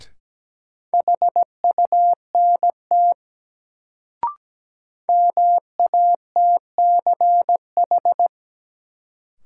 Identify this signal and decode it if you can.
Morse code, heard as a keyed tone, about 17 wpm, 700 Hz. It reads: HUNT MATCH
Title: Sentences from Top 1600 Words - New Word Drill - 17wpm